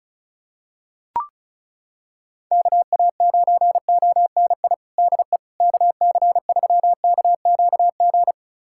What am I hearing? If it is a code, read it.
Morse code, heard as a keyed tone, about 34 wpm, 700 Hz. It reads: KA9ONI DE KC3KQG
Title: USA Callsign DE Callsign - 6 Characters - 35wpm